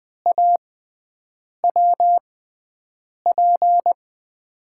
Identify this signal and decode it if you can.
Morse code, heard as a keyed tone, about 20 wpm, spaced as for 7 wpm, 700 Hz. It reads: AWP
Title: USA Call Signs - 5 Characters - 6x Character Spacing - 20wpm